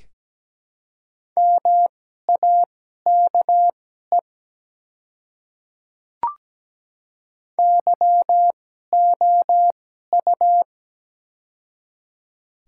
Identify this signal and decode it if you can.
Morse code, heard as a keyed tone, about 17 wpm, spaced as for 12 wpm, 700 Hz. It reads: MAKE YOU
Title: 2x Character Spacing - Top 100 Words - 17wpm